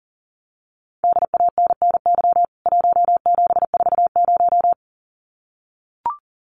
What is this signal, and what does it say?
Morse code, heard as a keyed tone, about 40 wpm, 700 Hz. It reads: DANNY 1740
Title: CWT Exchange 40wpm